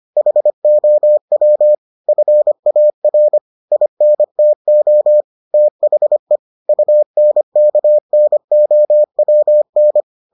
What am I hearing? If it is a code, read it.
Morse code, heard as a keyed tone, about 25 wpm, 600 Hz. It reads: HOW FAR INTO THE UNKNOWN